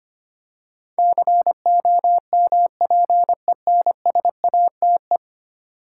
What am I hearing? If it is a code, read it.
Morse code, heard as a keyed tone, about 25 wpm, 700 Hz. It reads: COMPENSATE
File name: 10 Letter Words - First 3 Letters Given - 25wpm